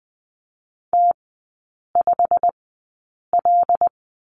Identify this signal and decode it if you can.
Morse code, heard as a keyed tone, about 20 wpm, spaced as for 8 wpm, 700 Hz. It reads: T5L
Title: 093 Lesson 16e - Letter c - Three Characters 20wpm